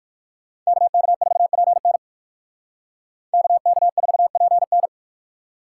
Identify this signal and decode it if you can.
Morse code, heard as a keyed tone, about 58 wpm, 700 Hz. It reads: KK4PN KK4PN